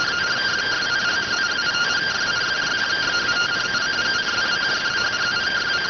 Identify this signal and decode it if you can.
Two-tone teleprinter signal, about 170 Hz shift.